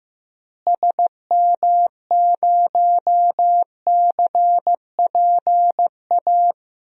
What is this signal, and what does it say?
Morse code, heard as a keyed tone, about 15 wpm, 700 Hz. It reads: SM0CPA
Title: DXCC-100 Call Signs - 6 Characters - Repeated 3x - 15wpm